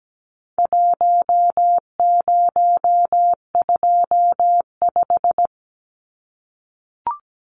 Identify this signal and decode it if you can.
Morse code, heard as a keyed tone, about 17 wpm, 700 Hz. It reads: 1025
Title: Numbers - Contest Style - 4-Digit 17wpm